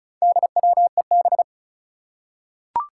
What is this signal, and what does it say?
Morse code, heard as a keyed tone, about 36 wpm, 700 Hz. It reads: DWEB
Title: Four Challenging Letter-Number Combinations 35wpm